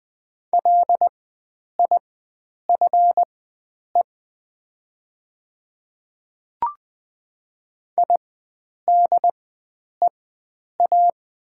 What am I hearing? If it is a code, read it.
Morse code, heard as a keyed tone, about 20 wpm, spaced as for 9 wpm, 700 Hz. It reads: LIFE IDEA
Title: 4x Character Spacing - Top 300 Words - 20wpm